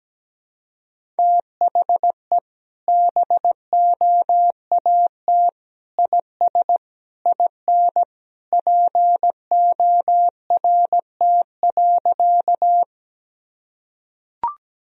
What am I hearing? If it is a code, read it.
Morse code, heard as a keyed tone, about 17 wpm, 700 Hz. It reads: THE BOAT IS IN PORT.
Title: Sentences from Top 1000 Words - Review Part 1 - 17wpm